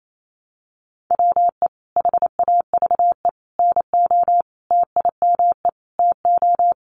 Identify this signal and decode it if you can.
Morse code, heard as a keyed tone, about 28 wpm, 700 Hz. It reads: WE HAVE NO TIME TO
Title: Sentences from Top 1100 Words 28wpm